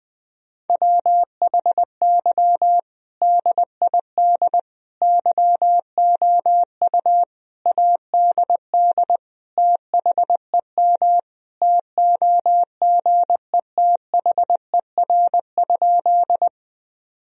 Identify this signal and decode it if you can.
Morse code, heard as a keyed tone, about 19 wpm, 700 Hz. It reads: WHY DID YOU ADD THEM TOGETHER?
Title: Sentences from Top 1000 Words - Review Part 1 - 20wpm